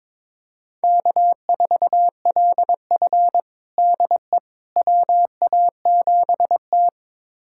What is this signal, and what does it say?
Morse code, heard as a keyed tone, about 22 wpm, 700 Hz. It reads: K4LF DE WA7T
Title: USA Callsign DE Callsign - 4 Characters - 22wpm